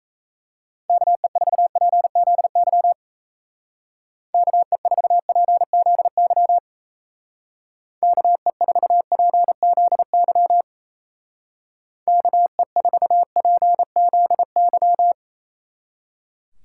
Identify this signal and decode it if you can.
Morse code, heard as a keyed tone, about 34 wpm, 700 Hz. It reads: KE4PZY KE4PZY KE4PZY KE4PZY
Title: Speed Racing - USA 2x3 Call Signs 28wpm